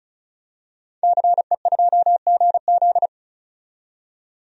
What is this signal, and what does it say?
Morse code, heard as a keyed tone, about 36 wpm, 700 Hz. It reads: CE2GZ
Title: DXCC-100 Call Signs - 5 Characters - 35wpm